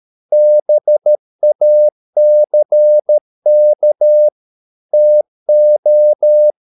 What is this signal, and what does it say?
Morse code, heard as a keyed tone, about 13 wpm, 600 Hz. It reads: BACK TO